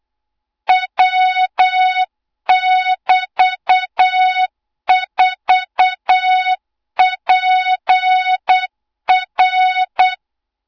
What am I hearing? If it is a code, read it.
Morse code, heard as a keyed tone, about 8 wpm, 3700 Hz, with noise. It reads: W<BT>4PR
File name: Morze-HA2MN / hang / 20 GYAKORLAT